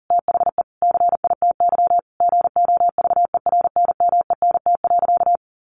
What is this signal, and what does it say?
Morse code, heard as a keyed tone, about 40 wpm, 700 Hz. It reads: THE CITY GOVERNMENT.
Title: Sentences from Top 1400 Words 40wpm